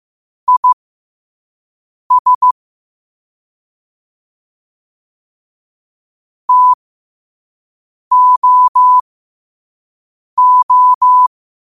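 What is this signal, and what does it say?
Morse code, heard as a keyed tone, about 15 wpm, spaced as for 5 wpm, 1000 Hz. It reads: IS TOO
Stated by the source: Morse code practice at 5 wpm.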